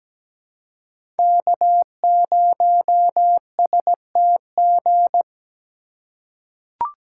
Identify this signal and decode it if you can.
Morse code, heard as a keyed tone, about 17 wpm, 700 Hz. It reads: K0STG
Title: USA Call Signs - 5 Characters - 2x Character Spacing - 17wpm